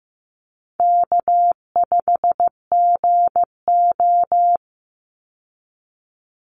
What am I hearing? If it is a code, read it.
Morse code, heard as a keyed tone, about 15 wpm, 700 Hz. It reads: K5GO
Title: Call Signs Repeated 2x - 15wpm